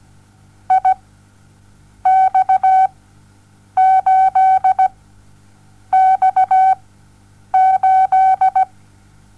Morse code, heard as a keyed tone, about 17 wpm, spaced as for 7 wpm, 760 Hz. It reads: IX8X8
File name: Onlinelehrgang / morsen / DJ4UF Morsen07